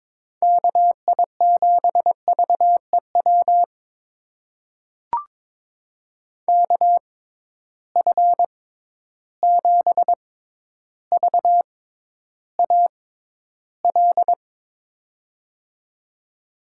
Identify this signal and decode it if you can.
Morse code, heard as a keyed tone, about 22 wpm, spaced as for 11 wpm, 700 Hz. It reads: KI7VEW KF7VAL
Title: USA Call Signs - 6 Characters - 6x Character Spacing - 22wpm